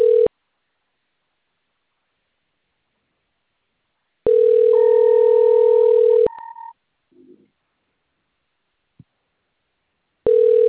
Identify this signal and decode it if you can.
Ringback tone (North America), 440 + 480 Hz.